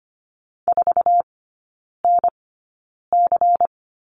Morse code, heard as a keyed tone, about 25 wpm, spaced as for 9 wpm, 700 Hz. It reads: 4NC
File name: course / 25:10wpm / 093 Lesson 16e - Letter c - Three Characters 25wpm